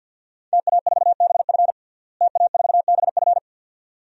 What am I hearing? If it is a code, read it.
Morse code, heard as a keyed tone, about 51 wpm, 700 Hz. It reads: TA4BF TA4BF
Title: DXCC-100 Call Signs - 5 Characters - Repeated 2x - 50wpm